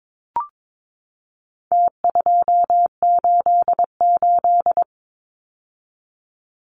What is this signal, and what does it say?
Morse code, heard as a keyed tone, about 22 wpm, 700 Hz. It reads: T288
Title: Numbers - Contest Style - 4-Digit 22wpm